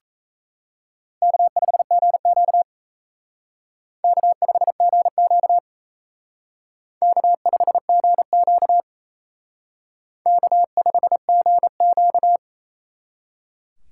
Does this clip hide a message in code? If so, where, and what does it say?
Yes: Morse code from 1.2 to 12.4 s, heard as a keyed tone, about 36 wpm, 700 Hz. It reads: K5GQ K5GQ K5GQ K5GQ